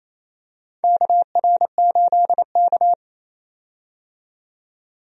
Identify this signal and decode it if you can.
Morse code, heard as a keyed tone, about 28 wpm, 690 Hz. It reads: KR8K
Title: USA Call Signs - 4 Characters - 28wpm